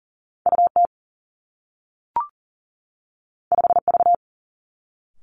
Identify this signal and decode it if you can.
Morse code, heard as a keyed tone, about 38 wpm, 700 Hz. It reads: U T 5 V
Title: Two Challenging Letter-Number Combinations 40wpm